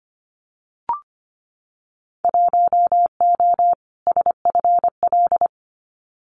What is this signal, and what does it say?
Morse code, heard as a keyed tone, about 25 wpm, 700 Hz. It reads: 1O SFL
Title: Winter Field Day Exchanges 25wpm